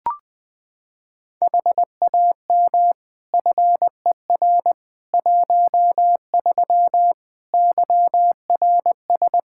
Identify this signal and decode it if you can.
Morse code, heard as a keyed tone, about 20 wpm, 700 Hz. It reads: HAM FER 13 YRS
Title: Practice QSO - Exchange 3 - More Info - 20wpm